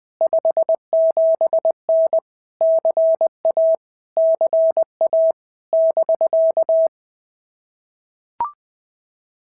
Morse code, heard as a keyed tone, about 20 wpm, 650 Hz. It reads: 57N CA CA <BK>
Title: POTA QSO Practice - 20wpm